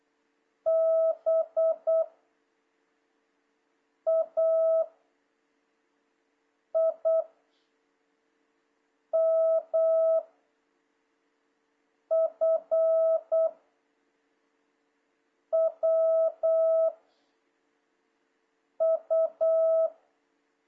Morse code, heard as a keyed tone, about 8 wpm, 650 Hz. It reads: B A I M F W U
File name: jh8cbh / 2026cw